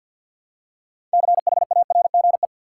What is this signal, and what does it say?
Morse code, heard as a keyed tone, about 52 wpm, 700 Hz. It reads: CHARGE
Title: Six Letter Words - 50wpm